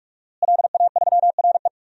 Morse code, heard as a keyed tone, about 45 wpm, 700 Hz. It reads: LA3RE
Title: DXCC-100 Call Signs - 5 Characters - Repeated 2x - 45wpm